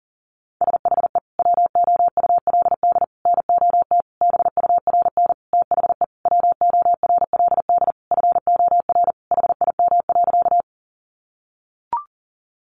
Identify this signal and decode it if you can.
Morse code, heard as a keyed tone, about 40 wpm, 700 Hz. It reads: SHE WOULD NOT BURN THE WORLD FOR HIM.
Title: Sentences from Top 1300 Words 40wpm